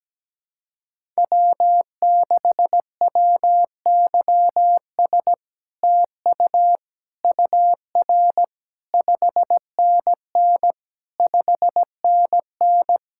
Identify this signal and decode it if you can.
Morse code, heard as a keyed tone, about 17 wpm, 700 Hz. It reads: W6WYS TU UR 5NN 5NN
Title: SOTA QSO Practice 17wpm